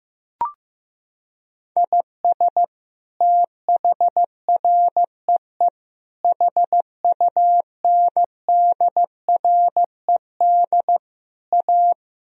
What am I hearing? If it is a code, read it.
Morse code, heard as a keyed tone, about 16 wpm, 700 Hz. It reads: IS THREE HUNDRED A
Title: Sentences from Top 1000 Words - Review Part 2 - 15wpm